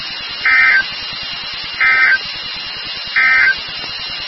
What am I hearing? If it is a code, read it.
Emergency Alert System (SAME) tones, heard as harsh buzzing bursts. It is heard from 0.3 to 3.6 s.
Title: NOAA Weather Radio WXK36